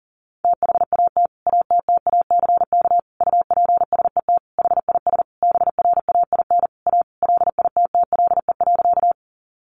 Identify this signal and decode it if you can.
Morse code, heard as a keyed tone, about 40 wpm, 700 Hz. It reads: THAT ATTACK UPSET HIS BRAIN A LITTLE.
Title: Sentences from Top 1500 Words 40wpm